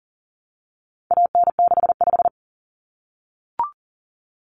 Morse code, heard as a keyed tone, about 40 wpm, 700 Hz. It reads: AN65
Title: Common Cut Numbers - 4-Digit 40wpm